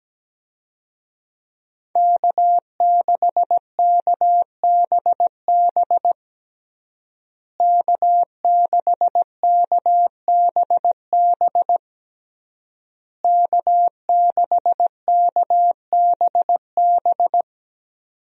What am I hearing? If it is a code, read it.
Morse code, heard as a keyed tone, about 17 wpm, 700 Hz. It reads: K6KBB K6KBB K6KBB
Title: USA Call Signs - 5 Characters - Repeated 3x - 17wpm